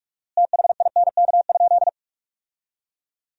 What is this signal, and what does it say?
Morse code, heard as a keyed tone, about 47 wpm, 700 Hz. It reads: THINK?
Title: 4x Word-Spacing - Sentences from Top 200 Words - 45wpm